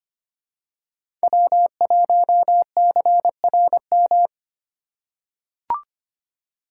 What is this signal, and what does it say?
Morse code, heard as a keyed tone, about 25 wpm, 700 Hz. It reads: W1CRM
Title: USA Call Signs - 5 Characters - 4x Character Spacing - 25wpm